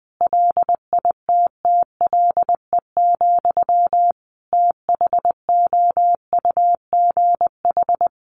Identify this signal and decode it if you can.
Morse code, heard as a keyed tone, about 20 wpm, 700 Hz. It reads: LITTLE, THOUGH
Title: Sentences from Top 400 Words 20wpm